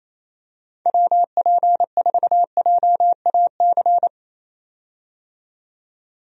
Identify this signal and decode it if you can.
Morse code, heard as a keyed tone, about 28 wpm, 700 Hz. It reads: WP4JAC
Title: USA Call Signs - 6 Characters - 28wpm